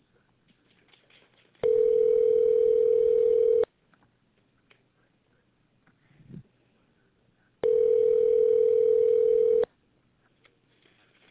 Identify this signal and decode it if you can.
Ringback tone (North America), 440 + 480 Hz.